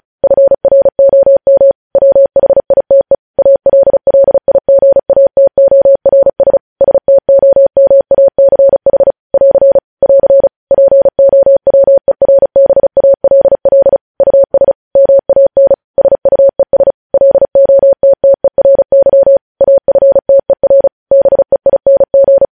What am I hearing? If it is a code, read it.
Morse code, heard as a keyed tone, about 36 wpm, 550 Hz. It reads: FROM WHITE ALLIGATORS STOMACH <AR> <AR> POWERBALL US MAN SUES LOTTERY AFTER BEING